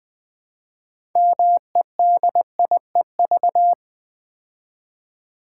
Morse code, heard as a keyed tone, about 20 wpm, 700 Hz. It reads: MEDIEV
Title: Build A Word - 3 Letters At A Time - 8-Letter Words - 20wpm